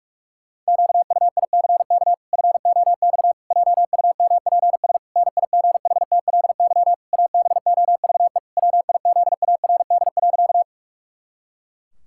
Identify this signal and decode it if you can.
Morse code, heard as a keyed tone, about 48 wpm, 700 Hz. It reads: QUICK FOX JUMPS NIGHTLY ABOVE WIZARD.
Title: Pangram Challenge v2 - 45wpm